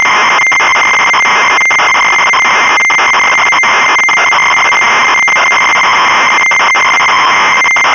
Dial-up modem handshake.